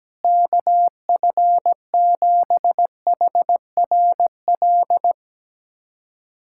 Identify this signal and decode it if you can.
Morse code, heard as a keyed tone, about 17 wpm, 700 Hz. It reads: KF7HRL
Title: USA Callsign DE Callsign - 6 Characters - 17wpm